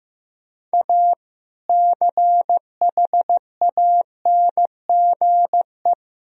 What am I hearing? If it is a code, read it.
Morse code, heard as a keyed tone, about 15 wpm, 700 Hz. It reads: A CHANGE
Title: Sentences from Top 1000 Words - Review Part 2 - 15wpm